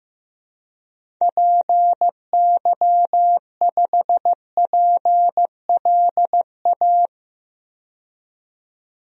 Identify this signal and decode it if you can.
Morse code, heard as a keyed tone, about 15 wpm, 700 Hz. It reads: PY5PLA
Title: DXCC-100 Call Signs - 6 Characters - 15wpm